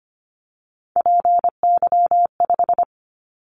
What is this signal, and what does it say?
Morse code, heard as a keyed tone, about 25 wpm, 700 Hz. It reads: PY5
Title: Callsign Prefixes for Top 100 DXCC - 25wpm